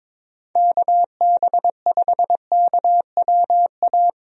Morse code, heard as a keyed tone, about 22 wpm, 700 Hz. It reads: KB5KWA
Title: USA Call Signs - 6 Characters - Repeated 3x - 22wpm